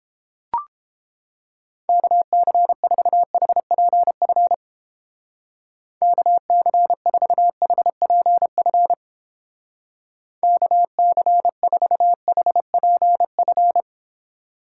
Morse code, heard as a keyed tone, about 29 wpm, 700 Hz. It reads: KC4HPF KC4HPF KC4HPF